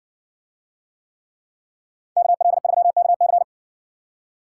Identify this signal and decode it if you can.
Morse code, heard as a keyed tone, about 55 wpm, 700 Hz. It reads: KL3KC